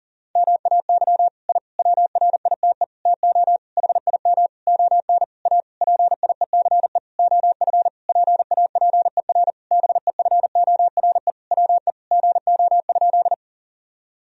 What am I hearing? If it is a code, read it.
Morse code, heard as a keyed tone, about 40 wpm, 700 Hz. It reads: MAY I WRITE TO HIM ON A PIECE OF PAPER BEFORE WE GO?
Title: Sentences from Top 1000 Words - Review Part 2 - 40wpm